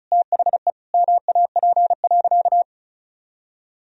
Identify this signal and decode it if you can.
Morse code, heard as a keyed tone, about 34 wpm, 700 Hz. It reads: THE MAP.
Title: Sentences from Top 1000 Words - Review Part 2 - 35wpm